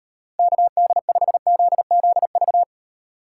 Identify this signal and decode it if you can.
Morse code, heard as a keyed tone, about 38 wpm, 700 Hz. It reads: KD5ZZV